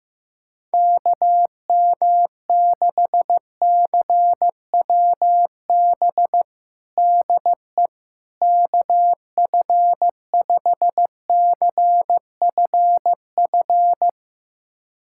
Morse code, heard as a keyed tone, about 16 wpm, 700 Hz. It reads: KM6CWB DE KF5CFF
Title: USA Callsign DE Callsign 15wpm